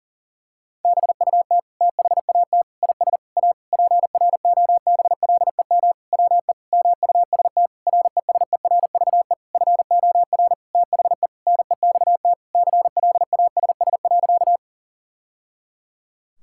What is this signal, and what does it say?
Morse code, heard as a keyed tone, about 40 wpm, 700 Hz. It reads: BUT THAT IS A PROBLEM WE MUST RESERVE FOR THE NEXT CLASS.
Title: Sentences from Top 1000 Words - Review Part 6 - 40wpm